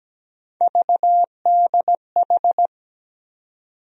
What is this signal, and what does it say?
Morse code, heard as a keyed tone, about 17 wpm, 700 Hz. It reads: VDH
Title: USA Call Sign Postfixes - 3 Letters 17wpm